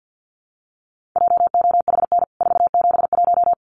Morse code, heard as a keyed tone, about 50 wpm, 700 Hz. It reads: JOHN 471
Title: CWT Exchange 50wpm